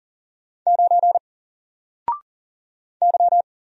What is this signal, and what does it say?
Morse code, heard as a keyed tone, about 38 wpm, 700 Hz. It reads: 9 Y
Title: Single Letter-Number Alt 40wpm